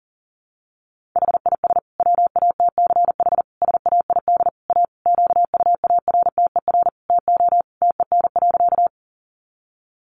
Morse code, heard as a keyed tone, about 40 wpm, 700 Hz. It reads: HIS WATCH SAID A QUARTER TO TEN.
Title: Sentences from Top 1300 Words 40wpm